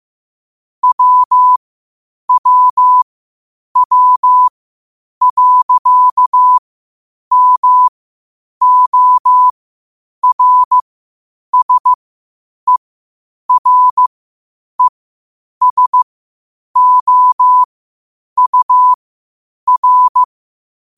Morse code, heard as a keyed tone, about 15 wpm, spaced as for 8 wpm, 1000 Hz. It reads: WWW.MORSERESOUR
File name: qotdfiles / 8wpm